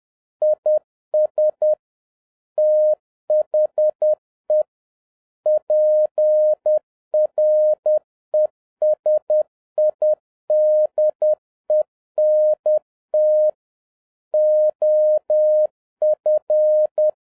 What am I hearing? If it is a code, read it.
Morse code, heard as a keyed tone, about 10 wpm, 610 Hz. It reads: IS THE PRESIDENT OF